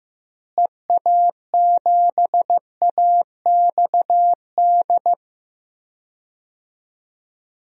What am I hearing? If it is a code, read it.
Morse code, heard as a keyed tone, about 15 wpm, 700 Hz. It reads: EA7AXD
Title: DXCC-100 Call Signs - 6 Characters - Repeated 3x - 15wpm